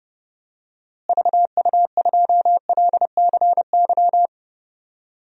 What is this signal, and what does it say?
Morse code, heard as a keyed tone, about 30 wpm, 700 Hz. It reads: VU2LCY
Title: DXCC-100 Call Signs - 6 Characters - Repeated 2x - 30wpm